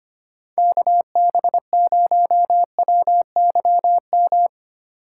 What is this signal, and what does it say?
Morse code, heard as a keyed tone, about 30 wpm, 700 Hz. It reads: KB0WYM
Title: Speed Racing - USA 2x3 Call Signs 25wpm